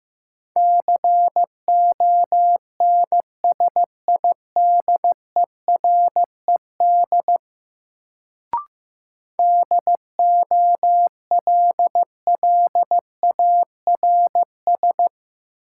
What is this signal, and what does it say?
Morse code, heard as a keyed tone, about 15 wpm, 700 Hz. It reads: CONSIDERED DOLLARS
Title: Sets of 5 Words - New Word Drill - 15wpm